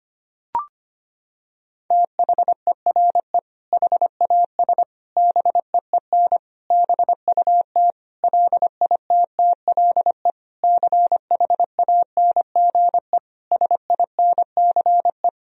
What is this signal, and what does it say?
Morse code, heard as a keyed tone, about 25 wpm, 700 Hz. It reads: THERE HAS BEEN BUT LITTLE CHANGE SINCE
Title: Sentences from Top 1000 Words - Review Part 1 - 25wpm